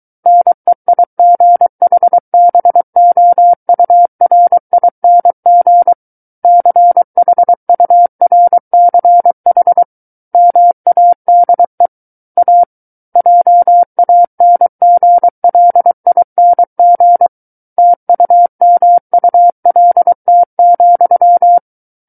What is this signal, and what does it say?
Morse code, heard as a keyed tone, about 23 wpm, 700 Hz. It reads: NEIGHBOURING CHURCH MADE A JANGLING TUMULT,
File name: Book One / 14-Chapter 14 - 23 WPM